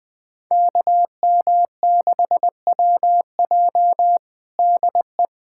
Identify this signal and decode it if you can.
Morse code, heard as a keyed tone, about 20 wpm, 700 Hz. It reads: KM6WJ DE
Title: USA Callsign DE Callsign 20wpm